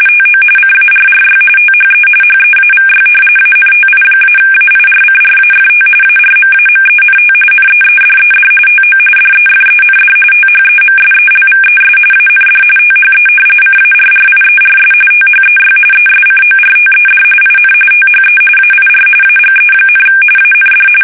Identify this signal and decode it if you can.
RTTY, about 45.45 baud. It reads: TO CARRY THE HEAVY ANTI-AIRCRAFT ARMAMENT NEEDED TO DEFEND THEMSELVES AND THE CARRIERS THEY ESCORTED IN THE AIR AGE, AND
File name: fsk850 / battleshipnj